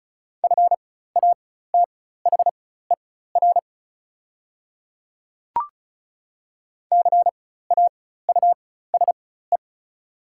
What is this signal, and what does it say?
Morse code, heard as a keyed tone, about 36 wpm, spaced as for 16 wpm, 700 Hz. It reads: FATHER CAUSE
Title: 4x Character Spacing - Top 300 Words - 35wpm